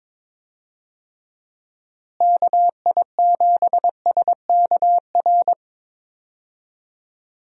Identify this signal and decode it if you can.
Morse code, heard as a keyed tone, about 22 wpm, 700 Hz. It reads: KI7SKR
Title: USA Call Signs - 6 Characters - Repeated 2x - 22wpm